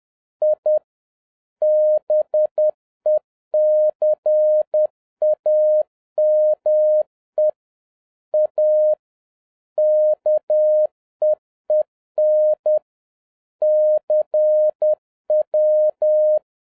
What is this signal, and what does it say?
Morse code, heard as a keyed tone, about 10 wpm, 610 Hz. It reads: I BECAME A KEEN CW